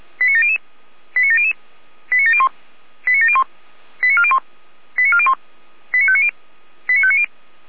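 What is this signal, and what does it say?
Five-tone selective call (ZVEI), heard as quick runs of five tones, tones of about 70 ms. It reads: 89700 89700 89701 89701 89401 89401 89590 89590
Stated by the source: ZVEI five-tone selective call received with an SDR by Feuerwehr Orlamünde.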